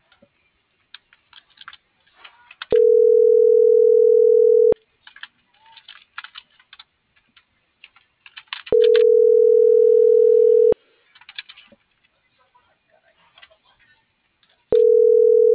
Ringback tone (North America), 440 + 480 Hz.